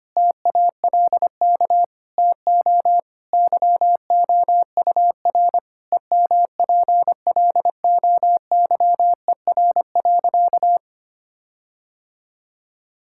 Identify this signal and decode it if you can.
Morse code, heard as a keyed tone, about 25 wpm, 700 Hz. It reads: TALK TO YOUR EMPLOYER.
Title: Sentences from Top 1600 Words 25wpm